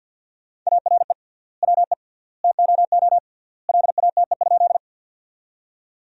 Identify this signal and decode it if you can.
Morse code, heard as a keyed tone, about 50 wpm, 700 Hz. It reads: ARE WE TOO LATE?
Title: 3x Word-Spacing - Sentences from Top 300 Words - 50wpm